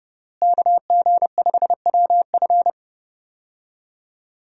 Morse code, heard as a keyed tone, about 30 wpm, 700 Hz. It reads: KG5WF
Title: USA Call Signs - 5 Characters - Repeated 3x - 30wpm